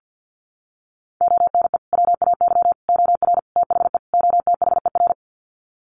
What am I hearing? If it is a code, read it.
Morse code, heard as a keyed tone, about 50 wpm, 700 Hz. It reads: ONE WAY OR THE OTHER
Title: Sets of 5 Words 50wpm